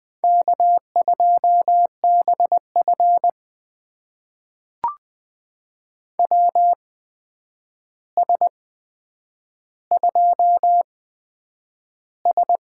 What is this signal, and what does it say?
Morse code, heard as a keyed tone, about 20 wpm, spaced as for 9 wpm, 700 Hz. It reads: K2BF WS2S
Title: USA Call Signs - 4 Characters - 8x Character Spacing - 20wpm